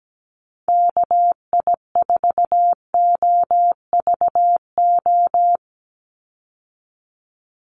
Morse code, heard as a keyed tone, about 17 wpm, 700 Hz. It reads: KI4OVO
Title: USA Call Signs 17wpm